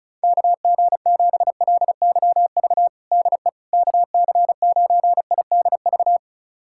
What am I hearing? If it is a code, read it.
Morse code, heard as a keyed tone, about 36 wpm, 700 Hz. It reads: KG7LYV DE KC9IDV
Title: USA Callsign DE Callsign - 6 Characters - 35wpm